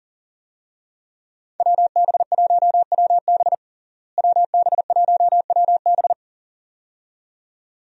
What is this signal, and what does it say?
Morse code, heard as a keyed tone, about 37 wpm, 700 Hz. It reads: WB1WB WB1WB
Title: USA Call Signs - 5 Characters - Repeated 2x - 40wpm